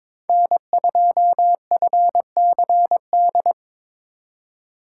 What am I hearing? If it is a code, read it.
Morse code, heard as a keyed tone, about 26 wpm, 700 Hz. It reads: N2FCD
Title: Speed Racing - USA 1x3 Call Signs 22wpm